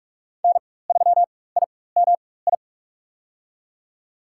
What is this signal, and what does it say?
Morse code, heard as a keyed tone, about 46 wpm, spaced as for 21 wpm, 700 Hz. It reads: N3IMI
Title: USA Call Signs - 4x Character Spacing - 45wpm